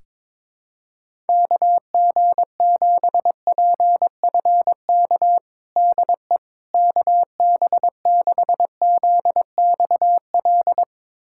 Morse code, heard as a keyed tone, about 21 wpm, 700 Hz. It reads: KG7PFK DE KB6ZXL
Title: USA Callsign DE Callsign - 6 Characters - 22wpm